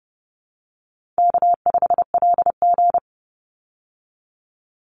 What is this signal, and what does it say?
Morse code, heard as a keyed tone, about 30 wpm, 700 Hz. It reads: K5LG
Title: Easy Call Signs Repeated 3x - 30wpm